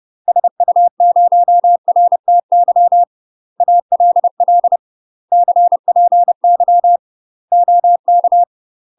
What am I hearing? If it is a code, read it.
Morse code, heard as a keyed tone, about 30 wpm, 700 Hz. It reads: SU0RTY ALL CPY OK
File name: UK MASTER QSOs TRAINING / 30WPM / UK MASTER QSOs 0046